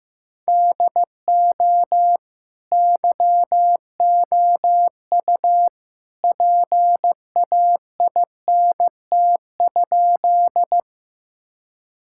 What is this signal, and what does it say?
Morse code, heard as a keyed tone, about 15 wpm, 700 Hz. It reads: DO YOU PAINT?
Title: Sentences from Top 1000 Words - Review Part 6 - 15wpm